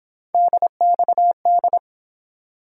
Morse code, heard as a keyed tone, about 29 wpm, 700 Hz. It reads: DXD